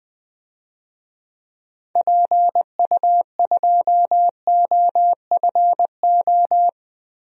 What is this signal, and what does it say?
Morse code, heard as a keyed tone, about 20 wpm, 700 Hz. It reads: PU2OFO
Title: DXCC-100 Call Signs - 6 Characters - Repeated 3x - 20wpm